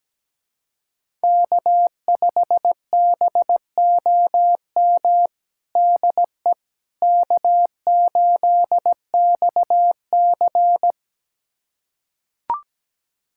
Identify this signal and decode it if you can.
Morse code, heard as a keyed tone, about 17 wpm, 700 Hz. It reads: K5BOM DE K8XC
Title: USA Callsign DE Callsign 17wpm